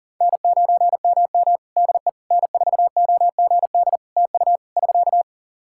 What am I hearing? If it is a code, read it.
Morse code, heard as a keyed tone, about 40 wpm, 700 Hz. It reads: N9MM DE N4OGD TU <SK>
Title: K1USN SST QSO Practice 40wpm